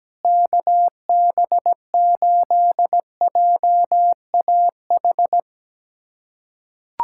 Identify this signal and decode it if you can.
Morse code, heard as a keyed tone, about 17 wpm, 700 Hz. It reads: KB8JAH
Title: USA Callsign DE Callsign 17wpm